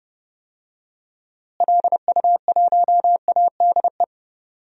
Morse code, heard as a keyed tone, about 30 wpm, 700 Hz. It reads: LU1ADE
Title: DXCC-100 Call Signs - 6 Characters - Repeated 2x - 30wpm